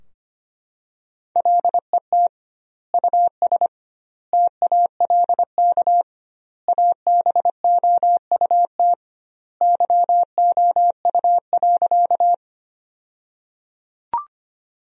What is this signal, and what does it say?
Morse code, heard as a keyed tone, about 25 wpm, 700 Hz. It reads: LET US TALK ABOUT YOU.
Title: 2x Word-Spacing - Sentences from Top 400 Words - 25wpm